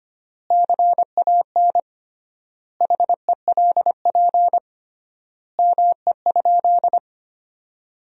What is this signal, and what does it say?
Morse code, heard as a keyed tone, about 25 wpm, 700 Hz. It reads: CAN HELP ME?
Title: 3x Word-Spacing - Sentences from Top 200 Words - 25wpm